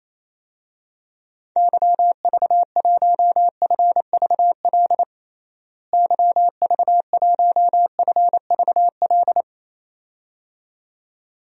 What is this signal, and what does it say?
Morse code, heard as a keyed tone, about 28 wpm, 700 Hz. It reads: YV1FVL YV1FVL
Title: DXCC-100 Call Signs - 6 Characters - Repeated 2x - 28wpm